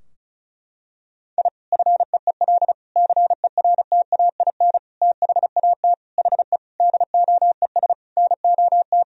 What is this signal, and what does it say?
Morse code, heard as a keyed tone, about 36 wpm, 700 Hz. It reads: I FEEL CERTAIN THAT HE DOES NOT
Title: Sentences from Top 1000 Words - Review Part 2 - 35wpm